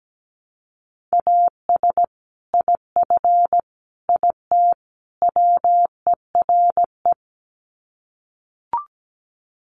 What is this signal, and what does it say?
Morse code, heard as a keyed tone, about 17 wpm, 700 Hz. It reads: AS IF IT WERE
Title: Sets of 4 Words 17wpm